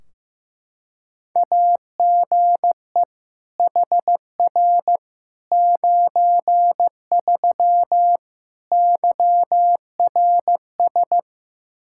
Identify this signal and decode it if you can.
Morse code, heard as a keyed tone, about 15 wpm, 700 Hz. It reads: AGE HR 93 YRS
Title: Practice QSO - Exchange 3 - More Info - 15wpm